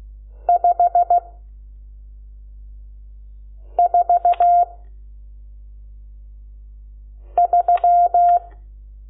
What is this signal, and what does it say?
Morse code, heard as a keyed tone, about 16 wpm, 700 Hz. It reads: 5 4 3
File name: Lecciones / CLASE Nº12